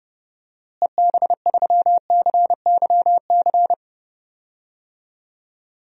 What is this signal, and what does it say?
Morse code, heard as a keyed tone, about 30 wpm, 700 Hz. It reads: EB3CYC
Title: DXCC-100 Call Signs - 6 Characters - Repeated 3x - 30wpm